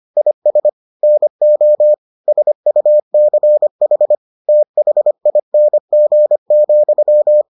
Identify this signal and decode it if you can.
Morse code, heard as a keyed tone, about 25 wpm, 600 Hz. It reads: IS NO SUCH THING,